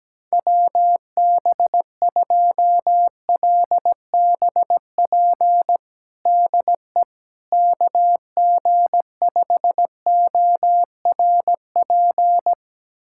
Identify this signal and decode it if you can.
Morse code, heard as a keyed tone, about 17 wpm, 700 Hz. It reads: WB2LBP DE KG5ORP
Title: USA Callsign DE Callsign - 6 Characters - 17wpm